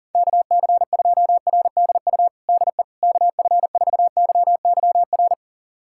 Morse code, heard as a keyed tone, about 40 wpm, 700 Hz. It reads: KC2RDU DE KF4YYR
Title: USA Callsign DE Callsign 40wpm